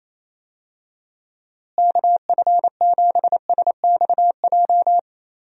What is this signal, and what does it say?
Morse code, heard as a keyed tone, about 28 wpm, 700 Hz. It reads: KF7SXJ
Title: USA Call Signs - 6 Characters - Repeated 2x - 28wpm